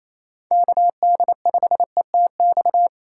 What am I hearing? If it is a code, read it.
Morse code, heard as a keyed tone, about 28 wpm, 700 Hz. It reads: KD5ETX
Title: USA Call Signs - 6 Characters - 3x Character Spacing - 28wpm